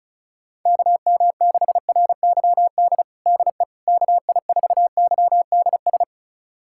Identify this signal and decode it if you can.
Morse code, heard as a keyed tone, about 34 wpm, 700 Hz. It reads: KM6RYD DE KI4YDS
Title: USA Callsign DE Callsign - 6 Characters - 35wpm